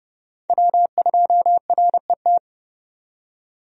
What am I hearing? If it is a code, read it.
Morse code, heard as a keyed tone, about 30 wpm, 700 Hz. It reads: W2RET
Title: USA Call Signs - 5 Characters - 30wpm